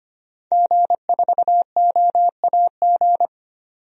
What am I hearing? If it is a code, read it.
Morse code, heard as a keyed tone, about 25 wpm, 700 Hz. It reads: G4OAG
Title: DXCC-100 Call Signs - 5 Characters - 25wpm